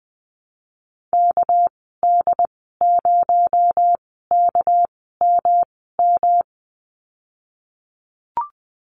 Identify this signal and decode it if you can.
Morse code, heard as a keyed tone, about 20 wpm, spaced as for 14 wpm, 700 Hz. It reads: KD0KMM
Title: 219 Lesson 35g - Number 0 - Call Signs 20wpm